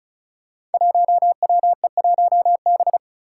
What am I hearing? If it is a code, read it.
Morse code, heard as a keyed tone, about 36 wpm, 700 Hz. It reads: 1WE1B
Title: Five Challenging Letter-Number Combinations 35wpm